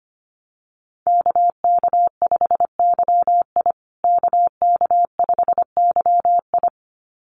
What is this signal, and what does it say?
Morse code, heard as a keyed tone, about 25 wpm, 700 Hz. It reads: KK5YI KK5YI
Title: USA Call Signs Repeated 2x 25wpm